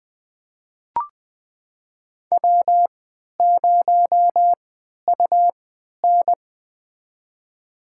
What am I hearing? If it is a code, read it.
Morse code, heard as a keyed tone, about 20 wpm, spaced as for 11 wpm, 700 Hz. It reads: W0UN
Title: USA Call Signs - 4 Characters - 3x Character Spacing - 20wpm